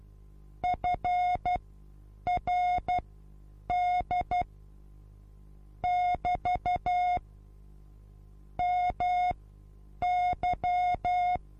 Morse code, heard as a keyed tone, about 12 wpm, spaced as for 8 wpm, 720 Hz. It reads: FRD <BT> MY
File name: Onlinelehrgang / morsen / DJ4UF Morsen23